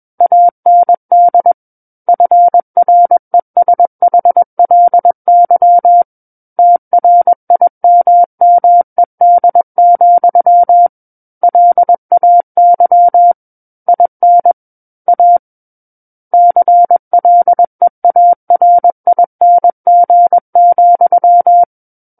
Morse code, heard as a keyed tone, about 21 wpm, 700 Hz. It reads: AND FRESHLY TRIMMED, LAY IN A CLEARING,
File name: Book One / 12-Chapter 12 - 21 WPM